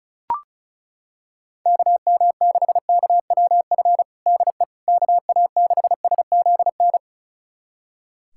Morse code, heard as a keyed tone, about 36 wpm, 700 Hz. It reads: KM6KWF DE KA6SZN
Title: USA Callsign DE Callsign - 6 Characters - 35wpm